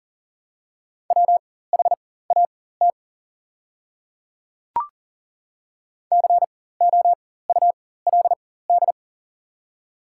Morse code, heard as a keyed tone, about 42 wpm, spaced as for 19 wpm, 690 Hz. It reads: WHAT COULD
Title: 4x Character Spacing - Top 100 Words - 40wpm